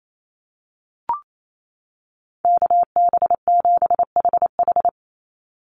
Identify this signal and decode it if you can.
Morse code, heard as a keyed tone, about 28 wpm, 700 Hz. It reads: KB7HH
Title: NA SOTA Chaser Callsigns 28wpm